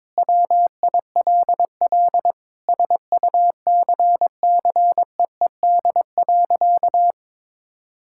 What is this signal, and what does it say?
Morse code, heard as a keyed tone, about 22 wpm, 700 Hz. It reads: WILL SUCCEED.
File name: Sentences with 9-Word Phrases - 22wpm